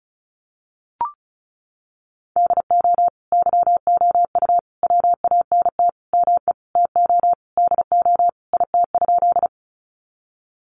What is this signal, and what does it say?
Morse code, heard as a keyed tone, about 34 wpm, 700 Hz. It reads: DO YOU WANT ME TO DO IT?
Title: Sentences from Top 200 Words 35wpm